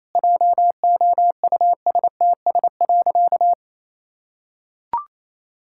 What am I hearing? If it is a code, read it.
Morse code, heard as a keyed tone, about 28 wpm, 700 Hz. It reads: JOUSTS.
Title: Pangram Challenge v2 - 28wpm